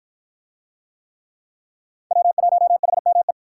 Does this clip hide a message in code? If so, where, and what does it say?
Yes: Morse code from 2.1 to 3.3 s, heard as a keyed tone, about 50 wpm, 700 Hz. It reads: W1HME